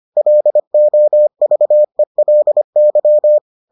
Morse code, heard as a keyed tone, about 25 wpm, 600 Hz. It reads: LOVELY